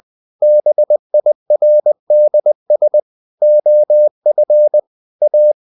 Morse code, heard as a keyed tone, about 20 wpm, 600 Hz. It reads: BIRDS OF A
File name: CW / Practice / EnglishSayings20